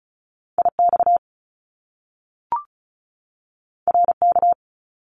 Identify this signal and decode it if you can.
Morse code, heard as a keyed tone, about 36 wpm, 700 Hz. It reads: IX RK
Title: Top Two Letter Combinations in Words 35wpm